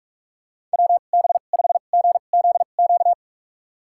Morse code, heard as a keyed tone, about 46 wpm, spaced as for 33 wpm, 690 Hz. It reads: WB5GZQ
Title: USA Call Signs - 6 Characters - 2x Character Spacing - 45wpm